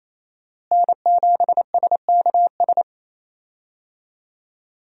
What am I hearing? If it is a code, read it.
Morse code, heard as a keyed tone, about 28 wpm, 700 Hz. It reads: N7SKS
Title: USA Call Signs - 5 Characters - Repeated 2x - 28wpm